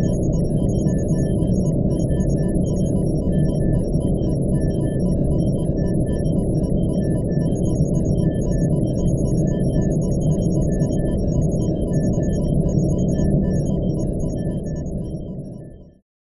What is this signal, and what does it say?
There is no code.